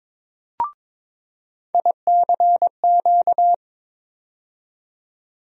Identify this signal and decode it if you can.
Morse code, heard as a keyed tone, about 22 wpm, 700 Hz. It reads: ICQ
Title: USA Call Sign Postfixes - 3 Letters 22wpm